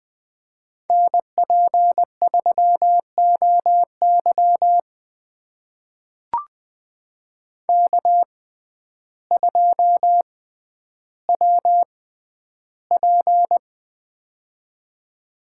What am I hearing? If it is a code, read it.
Morse code, heard as a keyed tone, about 20 wpm, spaced as for 11 wpm, 700 Hz. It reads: NP3OY K2WP
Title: USA Call Signs - 6x Character Spacing - 20wpm